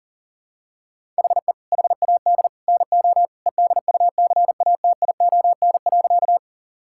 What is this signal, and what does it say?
Morse code, heard as a keyed tone, about 37 wpm, 700 Hz. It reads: HE HAD NO EDUCATION.
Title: Sentences from Top 1000 Words - Review Part 6 - 40wpm